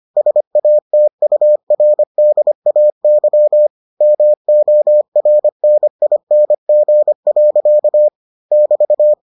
Morse code, heard as a keyed tone, about 25 wpm, 600 Hz. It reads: SATURDAY MORNING. <BT>